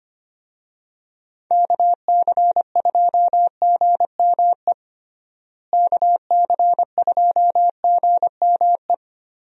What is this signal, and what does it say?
Morse code, heard as a keyed tone, about 25 wpm, 700 Hz. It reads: KC2GME KC2GME
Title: USA Call Signs - 6 Characters - Repeated 2x - 25wpm